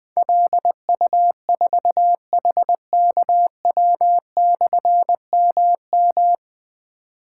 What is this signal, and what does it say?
Morse code, heard as a keyed tone, about 20 wpm, 700 Hz. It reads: LU4HKW/MM
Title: Difficult Call Signs 20wpm